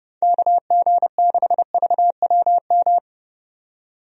Morse code, heard as a keyed tone, about 31 wpm, 700 Hz. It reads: KG6VWM
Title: USA Call Signs - 6 Characters - 30wpm